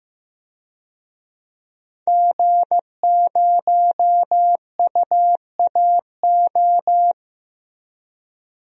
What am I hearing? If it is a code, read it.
Morse code, heard as a keyed tone, about 16 wpm, 700 Hz. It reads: G0UAO
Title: DXCC-100 Call Signs - 5 Characters - Repeated 2x - 15wpm